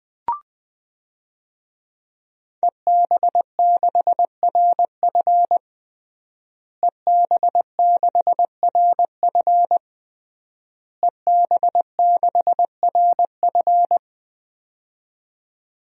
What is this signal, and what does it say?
Morse code, heard as a keyed tone, about 20 wpm, 700 Hz. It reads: EB6RF EB6RF EB6RF
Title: DXCC-100 Call Signs - 5 Characters - Repeated 3x - 20wpm